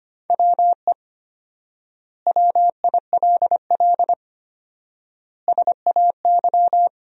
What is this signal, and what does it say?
Morse code, heard as a keyed tone, about 25 wpm, 700 Hz. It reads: WE WILL SAY
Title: 4x Word-Spacing - Sentences from Top 200 Words - 25wpm